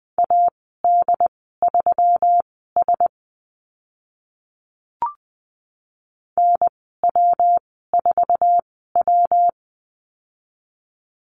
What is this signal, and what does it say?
Morse code, heard as a keyed tone, about 20 wpm, spaced as for 15 wpm, 700 Hz. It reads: AD3S NW4W
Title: 147 Lesson 24g - Letter y - Call Signs 20wpm